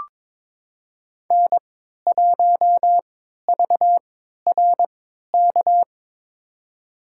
Morse code, heard as a keyed tone, about 22 wpm, spaced as for 13 wpm, 700 Hz. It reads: N1VRK
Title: USA Call Signs - 5 Characters - 3x Character Spacing - 22wpm